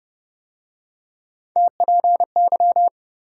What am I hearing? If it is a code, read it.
Morse code, heard as a keyed tone, about 33 wpm, 700 Hz. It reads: TPY